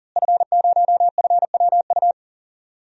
Morse code, heard as a keyed tone, about 40 wpm, 700 Hz. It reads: F0FWU
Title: DXCC-100 Call Signs - 5 Characters - Repeated 2x - 40wpm